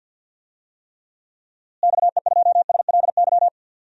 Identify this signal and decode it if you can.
Morse code, heard as a keyed tone, about 51 wpm, 700 Hz. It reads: XE2SLY
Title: DXCC-100 Call Signs - 6 Characters - Repeated 3x - 50wpm